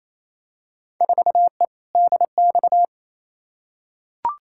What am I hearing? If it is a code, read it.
Morse code, heard as a keyed tone, about 28 wpm, 700 Hz. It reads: 4E DX
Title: Field Day Exchanges Encore - 28wpm